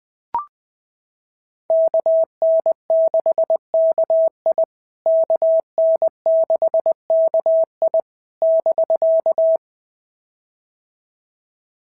Morse code, heard as a keyed tone, about 20 wpm, 650 Hz. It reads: KN6KI KN6KI <BK>
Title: POTA QSO Practice - 20wpm